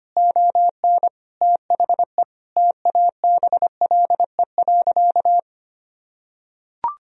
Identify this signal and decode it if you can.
Morse code, heard as a keyed tone, about 25 wpm, 700 Hz. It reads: ON THE TABLE.
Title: Sentences from Top 1000 Words - Review Part 3 - 25wpm